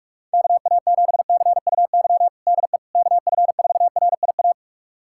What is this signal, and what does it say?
Morse code, heard as a keyed tone, about 46 wpm, 700 Hz. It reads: KA7CUY DE KF4RIA
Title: USA Callsign DE Callsign 45wpm